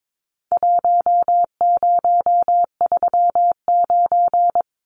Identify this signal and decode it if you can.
Morse code, heard as a keyed tone, about 22 wpm, 700 Hz. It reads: 1039
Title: Numbers - Contest Style - 4-Digit 22wpm